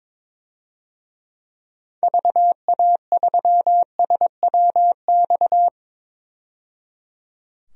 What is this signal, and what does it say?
Morse code, heard as a keyed tone, about 21 wpm, 700 Hz. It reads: VA3SWX
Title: DXCC-100 Call Signs - 6 Characters - 22wpm